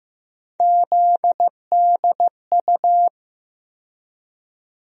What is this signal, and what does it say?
Morse code, heard as a keyed tone, about 15 wpm, 700 Hz. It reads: ZDU
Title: USA Call Sign Postfixes - 3 Letters 15wpm